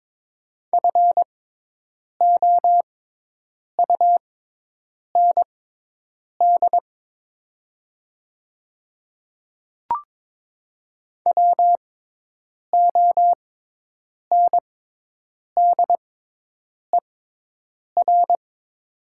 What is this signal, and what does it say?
Morse code, heard as a keyed tone, about 22 wpm, spaced as for 8 wpm, 700 Hz. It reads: FOUND WONDER
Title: 6x Character Spacing - Top 500 Words - 22wpm